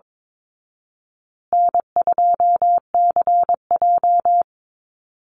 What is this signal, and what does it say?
Morse code, heard as a keyed tone, about 22 wpm, 700 Hz. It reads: N2CJ
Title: Easy Call Signs Repeated 3x - 22wpm